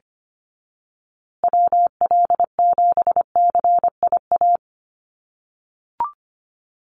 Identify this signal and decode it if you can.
Morse code, heard as a keyed tone, about 25 wpm, 700 Hz. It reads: WL7CIA
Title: USA Call Signs Repeated 2x 25wpm